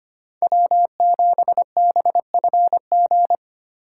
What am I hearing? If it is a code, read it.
Morse code, heard as a keyed tone, about 32 wpm, 700 Hz. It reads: W7BFG